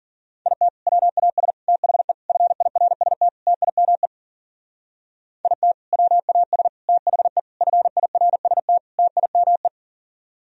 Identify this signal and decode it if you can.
Morse code, heard as a keyed tone, about 54 wpm, 700 Hz. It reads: IT WAS THE FIRST TIME IT WAS THE FIRST TIME